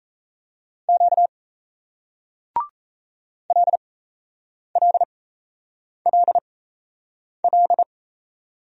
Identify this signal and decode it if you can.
Morse code, heard as a keyed tone, about 36 wpm, spaced as for 8 wpm, 700 Hz. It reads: Q LLLL